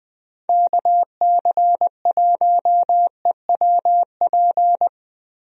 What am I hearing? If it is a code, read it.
Morse code, heard as a keyed tone, about 20 wpm, 700 Hz. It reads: KC1EWP
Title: USA Call Signs - 6 Characters - Repeated 2x - 20wpm